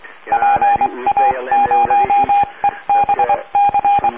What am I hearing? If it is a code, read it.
Morse code, heard as a keyed tone, about 23 wpm, 810 Hz, with radio noise. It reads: PA0EBC